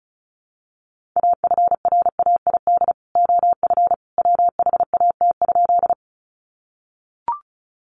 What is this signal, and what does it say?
Morse code, heard as a keyed tone, about 34 wpm, 700 Hz. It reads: AFRAID OF WHAT?
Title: Sentences from Top 1500 Words 35wpm